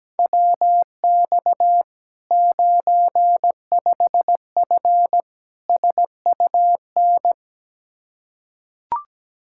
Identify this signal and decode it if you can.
Morse code, heard as a keyed tone, about 17 wpm, 700 Hz. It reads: WX 95F SUN
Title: Practice QSO - Exchange 2 - Info - 17wpm